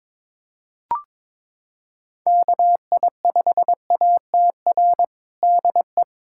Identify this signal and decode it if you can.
Morse code, heard as a keyed tone, about 22 wpm, 700 Hz. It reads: KI5ATR DE
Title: USA Callsign DE Callsign 22wpm